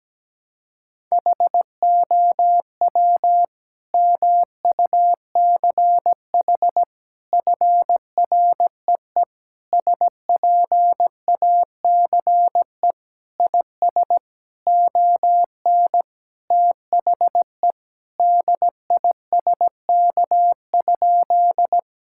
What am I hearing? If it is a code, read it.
Morse code, heard as a keyed tone, about 17 wpm, 700 Hz. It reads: HOW MUCH FREE SPACE IS ON THE DISK?
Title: Sentences from Top 1000 Words - Review Part 6 - 17wpm